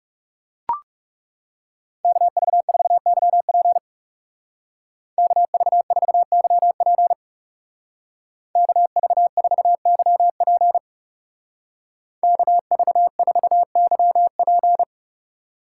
Morse code, heard as a keyed tone, about 37 wpm, 700 Hz. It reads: KV4YP KV4YP KV4YP KV4YP
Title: Speed Racing - USA Call Signs 30wpm